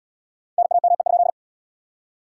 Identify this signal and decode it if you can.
Morse code, heard as a keyed tone, about 77 wpm, 700 Hz. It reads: NAME?